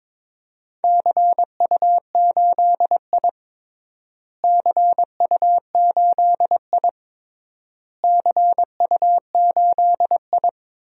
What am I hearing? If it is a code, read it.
Morse code, heard as a keyed tone, about 21 wpm, 700 Hz. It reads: CU8I CU8I CU8I
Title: DXCC-100 Call Signs - 4 Characters - Repeated 3x - 22wpm